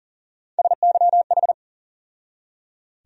Morse code, heard as a keyed tone, about 40 wpm, 700 Hz. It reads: SYH
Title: USA Call Sign Postfixes - 3 Letters 40wpm